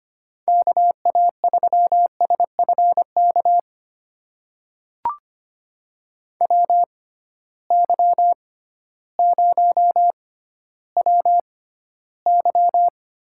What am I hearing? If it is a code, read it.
Morse code, heard as a keyed tone, about 25 wpm, spaced as for 14 wpm, 700 Hz. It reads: KA3SFK WY0WY
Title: USA Call Signs - 6 Characters - 6x Character Spacing - 25wpm